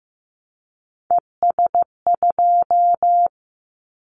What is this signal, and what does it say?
Morse code, heard as a keyed tone, about 15 wpm, 700 Hz. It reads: ES2
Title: Callsign Prefixes for Top 50 DXCC - 15wpm